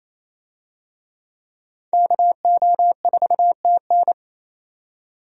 Morse code, heard as a keyed tone, about 28 wpm, 700 Hz. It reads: KO4TN
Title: USA Call Signs - 5 Characters - 28wpm